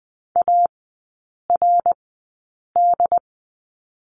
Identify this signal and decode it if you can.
Morse code, heard as a keyed tone, about 20 wpm, spaced as for 8 wpm, 700 Hz. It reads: ARD
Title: 067 Lesson 12e - Letter d - Three Characters 20wpm